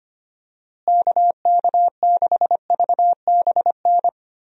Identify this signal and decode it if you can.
Morse code, heard as a keyed tone, about 25 wpm, 700 Hz. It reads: KK6VBN
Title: USA Call Signs - 6 Characters - Repeated 3x - 25wpm